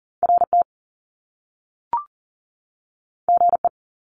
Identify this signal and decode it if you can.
Morse code, heard as a keyed tone, about 40 wpm, 700 Hz. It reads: RT GE
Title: Top Two Letter Combinations in Words 40wpm